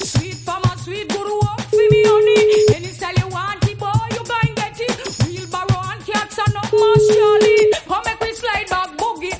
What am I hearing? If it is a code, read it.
Ringback tone (Europe, 425 Hz).